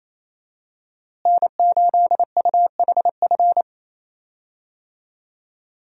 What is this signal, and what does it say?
Morse code, heard as a keyed tone, about 28 wpm, 700 Hz. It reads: N8UHF
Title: USA Call Signs - 5 Characters - Repeated 2x - 28wpm